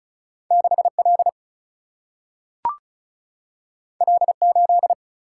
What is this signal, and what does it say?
Morse code, heard as a keyed tone, about 36 wpm, 700 Hz. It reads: 6L L8
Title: Single-Letter Single-Number 35wpm